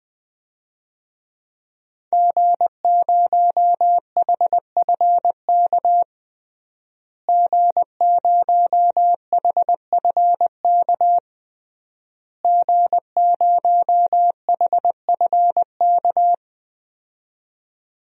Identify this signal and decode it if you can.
Morse code, heard as a keyed tone, about 20 wpm, 700 Hz. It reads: G0HFK G0HFK G0HFK
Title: DXCC-100 Call Signs - 5 Characters - Repeated 3x - 20wpm